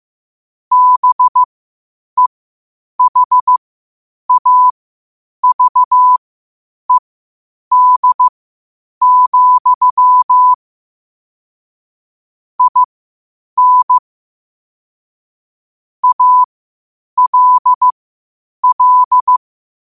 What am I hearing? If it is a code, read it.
Morse code, heard as a keyed tone, about 15 wpm, spaced as for 8 wpm, 1000 Hz. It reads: BEHAVED, IN ALL
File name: qotdfiles / 8wpm